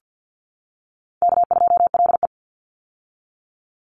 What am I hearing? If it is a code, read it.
Morse code, heard as a keyed tone, about 50 wpm, 700 Hz. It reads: K2LE
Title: Easy Call Signs - 50wpm